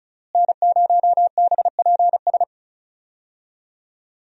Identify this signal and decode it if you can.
Morse code, heard as a keyed tone, about 34 wpm, 700 Hz. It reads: N0BPS
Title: USA Call Signs - 5 Characters - Repeated 2x - 35wpm